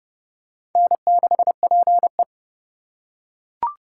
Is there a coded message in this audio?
Yes: Morse code from 0.8 to 2.2 s, heard as a keyed tone, about 29 wpm, 700 Hz. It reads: N6PE